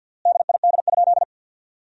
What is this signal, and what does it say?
Morse code, heard as a keyed tone, about 50 wpm, 700 Hz. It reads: DID?
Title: 3x Word-Spacing - Sentences from Top 100 Words - 50wpm